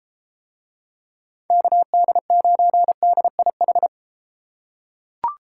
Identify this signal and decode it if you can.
Morse code, heard as a keyed tone, about 31 wpm, 700 Hz. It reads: KD9DIH